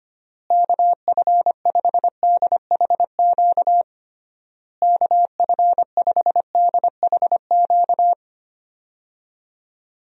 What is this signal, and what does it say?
Morse code, heard as a keyed tone, about 25 wpm, 700 Hz. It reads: KF5DHQ KF5DHQ
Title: USA Call Signs - 6 Characters - Repeated 3x - 25wpm